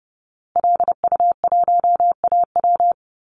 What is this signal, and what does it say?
Morse code, heard as a keyed tone, about 30 wpm, 700 Hz. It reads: LU1AW
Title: Call Signs Repeated 2x - 30wpm